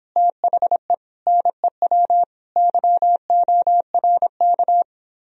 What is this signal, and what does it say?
Morse code, heard as a keyed tone, about 31 wpm, 700 Hz. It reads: THE NEW YORK